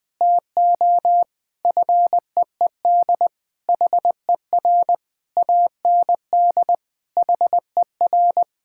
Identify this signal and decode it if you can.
Morse code, heard as a keyed tone, about 20 wpm, 700 Hz. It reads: TO FEED HER AND HER
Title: Sentences from Top 1600 Words 20wpm